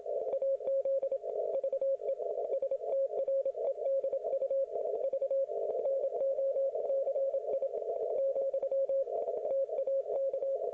Morse code, heard as a keyed tone, about 27 wpm, 540 Hz. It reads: AZ VE STREDU V NOCI , TAK